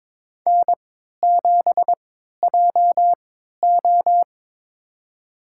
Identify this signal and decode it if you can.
Morse code, heard as a keyed tone, about 22 wpm, spaced as for 13 wpm, 700 Hz. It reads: N7JO
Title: USA Call Signs - 4 Characters - 3x Character Spacing - 22wpm